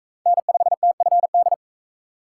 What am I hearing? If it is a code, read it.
Morse code, heard as a keyed tone, about 37 wpm, 700 Hz. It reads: N5TFD